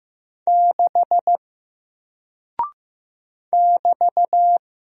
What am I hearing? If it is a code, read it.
Morse code, heard as a keyed tone, about 15 wpm, 700 Hz. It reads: 6 <BT>
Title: Single Character (Letter:Number:Punctuation:Prosign) Alt 15wpm